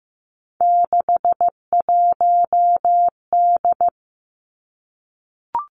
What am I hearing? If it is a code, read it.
Morse code, heard as a keyed tone, about 15 wpm, 700 Hz. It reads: 61D
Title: Three Challenging Letter-Number Combinations 15wpm